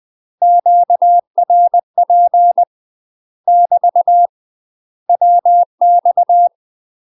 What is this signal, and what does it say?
Morse code, heard as a keyed tone, about 20 wpm, 700 Hz. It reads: QRP <BT> WX
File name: UK MASTER QSOs TRAINING/20WPM